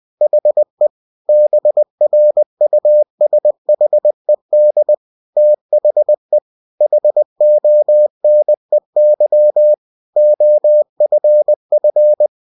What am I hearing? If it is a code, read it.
Morse code, heard as a keyed tone, about 20 wpm, 600 Hz. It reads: HE BRUSHED THE HONEY OFF
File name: cw-books / winnie-the-pooh / winnie-20wpm-page-0076